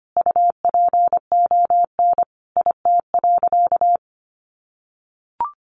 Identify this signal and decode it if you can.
Morse code, heard as a keyed tone, about 25 wpm, 700 Hz. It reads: UPON IT.
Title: Sentences from Top 1400 Words 25wpm